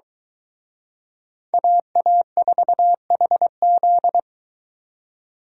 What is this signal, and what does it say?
Morse code, heard as a keyed tone, about 22 wpm, 700 Hz. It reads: AA4HZ